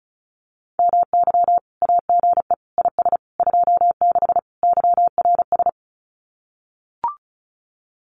Morse code, heard as a keyed tone, about 34 wpm, 700 Hz. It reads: MY AGE IS 26 YRS
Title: Common QSO Phrases Encore 35wpm